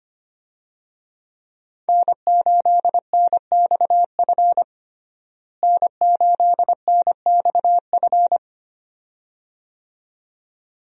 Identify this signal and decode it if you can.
Morse code, heard as a keyed tone, about 25 wpm, 700 Hz. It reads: N8NXF N8NXF
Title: USA Call Signs - 5 Characters - Repeated 2x - 25wpm